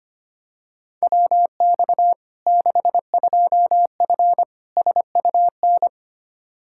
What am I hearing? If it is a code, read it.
Morse code, heard as a keyed tone, about 25 wpm, 700 Hz. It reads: WX 62F SUN
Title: Practice QSO - Exchange 2 - Info - 25wpm